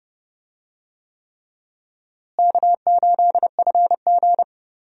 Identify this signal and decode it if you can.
Morse code, heard as a keyed tone, about 30 wpm, 700 Hz. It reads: K8FG
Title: USA Call Signs - 4 Characters - Repeated 3x - 30wpm